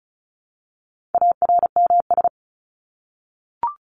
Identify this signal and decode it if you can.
Morse code, heard as a keyed tone, about 34 wpm, 700 Hz. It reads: ARMS
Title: ES:S Suffix - Top 100 Words with Base Words 35wpm